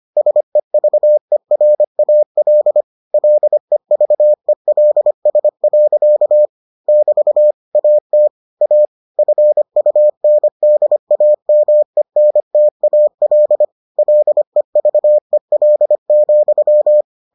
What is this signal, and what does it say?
Morse code, heard as a keyed tone, about 25 wpm, 600 Hz. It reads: SEVERAL LEVELS. <BT> AT A FUNDAMENTAL LEVEL,